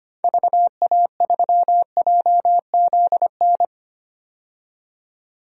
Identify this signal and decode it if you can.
Morse code, heard as a keyed tone, about 25 wpm, 700 Hz. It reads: VA3JZN
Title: DXCC-100 Call Signs - 6 Characters - 25wpm